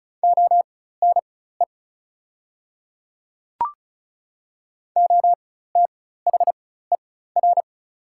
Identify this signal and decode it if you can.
Morse code, heard as a keyed tone, about 36 wpm, spaced as for 16 wpm, 690 Hz. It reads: ONE OTHER
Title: 4x Character Spacing - Top 100 Words - 35wpm